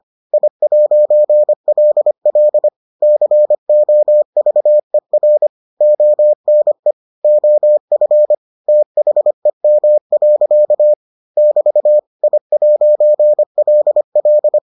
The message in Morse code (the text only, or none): I'LL COVER ONE OF THEM. <BT> I'LL